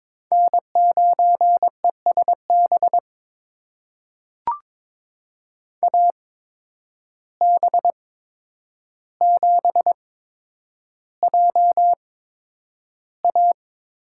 Morse code, heard as a keyed tone, about 22 wpm, spaced as for 9 wpm, 700 Hz. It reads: N9ESB AB7JA
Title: USA Call Signs - 5 Characters - 8x Character Spacing - 22wpm